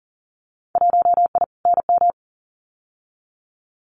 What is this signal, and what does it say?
Morse code, heard as a keyed tone, about 40 wpm, 700 Hz. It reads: 1I NM
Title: Winter Field Day Exchanges Encore 40wpm